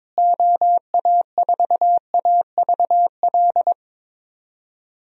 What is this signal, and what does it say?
Morse code, heard as a keyed tone, about 22 wpm, 700 Hz. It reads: OA4AVL
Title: USA Call Signs - 6 Characters - Repeated 3x - 22wpm